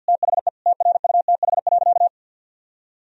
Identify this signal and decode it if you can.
Morse code, heard as a keyed tone, about 50 wpm, 700 Hz. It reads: THE TRUTH.
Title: Sentences from Top 1000 Words - Review Part 4 - 50wpm